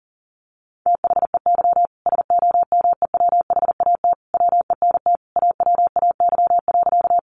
Morse code, heard as a keyed tone, about 40 wpm, 700 Hz. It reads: THEY SOMEWHAT WENT AWAY.
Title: Sentences from Top 1300 Words 40wpm